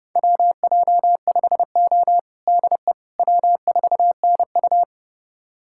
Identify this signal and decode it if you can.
Morse code, heard as a keyed tone, about 30 wpm, 700 Hz. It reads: WJ5O DE W4NU
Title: USA Callsign DE Callsign - 4 Characters - 30wpm